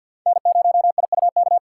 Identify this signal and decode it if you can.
Morse code, heard as a keyed tone, about 50 wpm, 700 Hz. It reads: N0IUK
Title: USA Call Signs - 4x Character Spacing - 50wpm